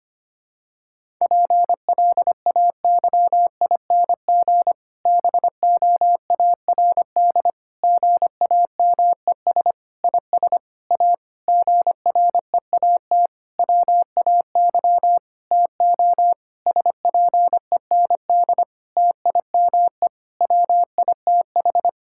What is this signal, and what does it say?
Morse code, heard as a keyed tone, about 25 wpm, 700 Hz. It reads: PLAYING BOARD GAMES IS A GREAT WAY TO SPEND TIME WITH
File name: practice / 25wpm / Difficult Sentences with Hints - 25wpm